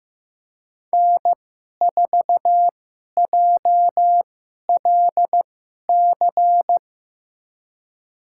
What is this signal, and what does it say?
Morse code, heard as a keyed tone, about 15 wpm, spaced as for 11 wpm, 700 Hz. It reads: N4JLC
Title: USA Call Signs - 5 Characters - 2x Character Spacing - 15wpm